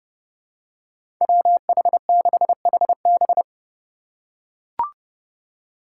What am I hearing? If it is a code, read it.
Morse code, heard as a keyed tone, about 30 wpm, 700 Hz. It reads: WH6HB
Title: USA Call Signs - 5 Characters - 4x Character Spacing - 30wpm